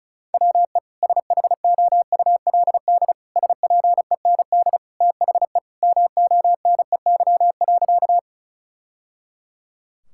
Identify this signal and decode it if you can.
Morse code, heard as a keyed tone, about 36 wpm, 700 Hz. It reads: WE SHOULD SPEND THE MONEY.
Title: Sentences from Top 1000 Words - Review Part 5 - 35wpm